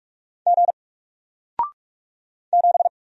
Morse code, heard as a keyed tone, about 48 wpm, 700 Hz. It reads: G 7
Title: Single Letter-Number Alt 45wpm